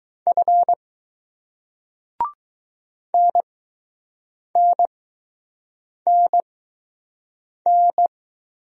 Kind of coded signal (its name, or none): Morse code